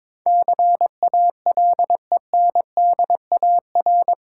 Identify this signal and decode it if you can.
Morse code, heard as a keyed tone, about 22 wpm, 700 Hz. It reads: CALENDAR
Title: Sentences from Top 1600 Words - New Word Drill - 22wpm